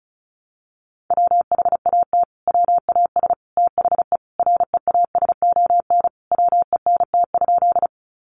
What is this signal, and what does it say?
Morse code, heard as a keyed tone, about 34 wpm, 700 Hz. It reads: WHAT WAS THE REASON WENT?
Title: Sentences from Top 700 Words 35wpm